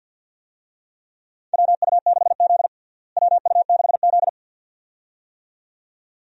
Morse code, heard as a keyed tone, about 51 wpm, 700 Hz. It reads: WU6Z WU6Z
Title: USA Call Signs - 4 Characters - Repeated 2x - 50wpm